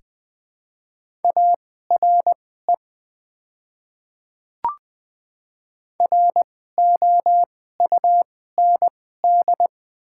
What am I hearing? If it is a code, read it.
Morse code, heard as a keyed tone, about 20 wpm, spaced as for 14 wpm, 690 Hz. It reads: ARE ROUND
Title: 2x Character Spacing - Top 500 Words - 20wpm